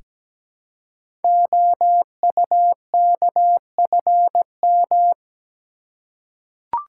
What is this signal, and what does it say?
Morse code, heard as a keyed tone, about 17 wpm, 700 Hz. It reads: OUKFM
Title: Code Groups - 5 Letters - 17wpm